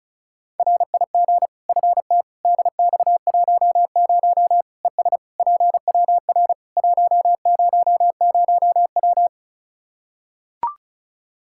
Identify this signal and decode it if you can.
Morse code, heard as a keyed tone, about 36 wpm, 700 Hz. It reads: RIG FT DX10 ES PWR 100W
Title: Practice QSO - Exchange 2 - Info - 35wpm